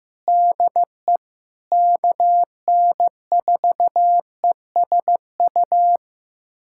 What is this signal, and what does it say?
Morse code, heard as a keyed tone, about 16 wpm, 700 Hz. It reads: DE KN4ESU
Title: USA Callsign DE Callsign 15wpm